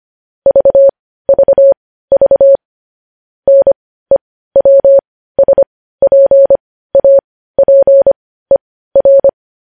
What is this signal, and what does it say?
Morse code, heard as a keyed tone, about 25 wpm, spaced as for 15 wpm, 550 Hz. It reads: VVV NEWSPAPER
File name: podcast / News headlines-1x-25x15